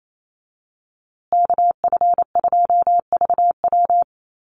Morse code, heard as a keyed tone, about 28 wpm, 700 Hz. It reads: KF2VW
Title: USA Call Signs Repeated 3x 28wpm